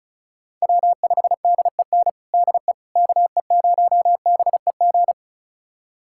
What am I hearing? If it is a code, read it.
Morse code, heard as a keyed tone, about 36 wpm, 700 Hz. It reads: W5DEN DE KE0BEG
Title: USA Callsign DE Callsign 35wpm